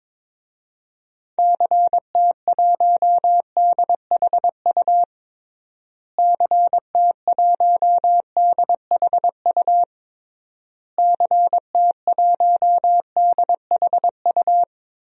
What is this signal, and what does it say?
Morse code, heard as a keyed tone, about 22 wpm, 700 Hz. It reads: CT1DHU CT1DHU CT1DHU
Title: DXCC-100 Call Signs - 6 Characters - Repeated 3x - 22wpm